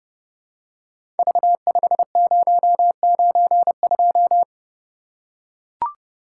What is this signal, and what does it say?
Morse code, heard as a keyed tone, about 30 wpm, 700 Hz. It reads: V5092
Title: Code Groups - 5 Letter:Numbers - 30wpm